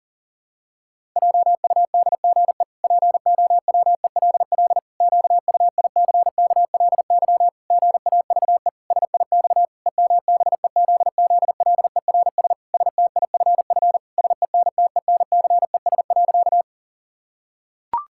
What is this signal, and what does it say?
Morse code, heard as a keyed tone, about 40 wpm, 700 Hz. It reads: JUDGE POWELL QUICKLY GAVE SIX EMBEZZLERS STIFF SENTENCES.
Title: Pangram Challenge v2 - 40wpm